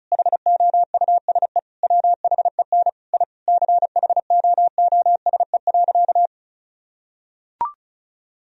Morse code, heard as a keyed tone, about 34 wpm, 700 Hz. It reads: HOUSE WHEN I CHOOSE.
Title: Sentences from Top 1000 Words - Review Part 4 - 35wpm